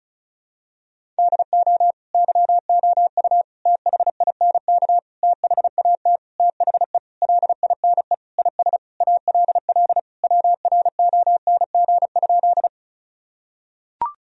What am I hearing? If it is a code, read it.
Morse code, heard as a keyed tone, about 34 wpm, 700 Hz. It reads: DO YOU THINK THAT THE LINE IS ALL WRONG?
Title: Sentences from Top 1000 Words - Review Part 1 - 35wpm